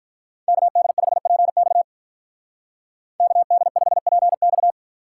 Morse code, heard as a keyed tone, about 45 wpm, 700 Hz. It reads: KD5PX KD5PX